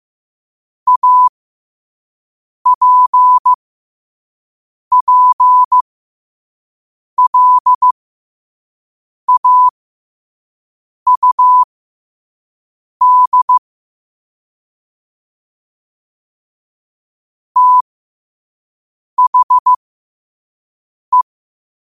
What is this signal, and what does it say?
Morse code, heard as a keyed tone, about 15 wpm, spaced as for 5 wpm, 1000 Hz. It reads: APPLAUD THE
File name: qotdfiles / 5wpm